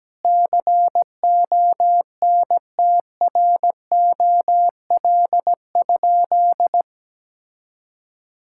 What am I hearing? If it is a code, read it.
Morse code, heard as a keyed tone, about 17 wpm, 700 Hz. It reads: CONTROL?
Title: Sentences from Top 1000 Words - Review Part 5 - 17wpm